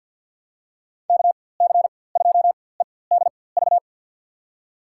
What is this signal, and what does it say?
Morse code, heard as a keyed tone, about 49 wpm, spaced as for 23 wpm, 700 Hz. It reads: KC2EDV
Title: USA Call Signs - 6 Characters - 4x Character Spacing - 50wpm